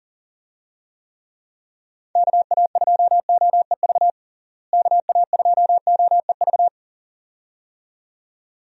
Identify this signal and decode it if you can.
Morse code, heard as a keyed tone, about 40 wpm, 700 Hz. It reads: KA2OEV KA2OEV
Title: USA Call Signs - 6 Characters - Repeated 2x - 40wpm